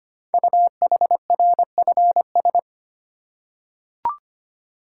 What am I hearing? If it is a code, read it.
Morse code, heard as a keyed tone, about 25 wpm, 700 Hz. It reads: UHRFS
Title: Code Groups - 5 Letters - 25wpm